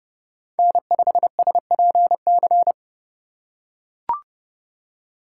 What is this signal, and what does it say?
Morse code, heard as a keyed tone, about 30 wpm, 700 Hz. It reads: N5SPC